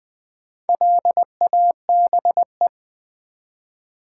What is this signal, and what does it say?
Morse code, heard as a keyed tone, about 20 wpm, 700 Hz. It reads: LABE
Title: Build A Word - 1 Letter At A Time - 5-Letter Words - 20wpm